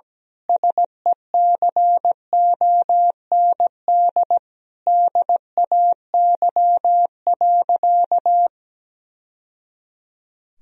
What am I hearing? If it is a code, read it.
Morse code, heard as a keyed tone, about 17 wpm, 700 Hz. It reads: SECOND DAY.
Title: Sentences from Top 1000 Words - Review Part 4 - 17wpm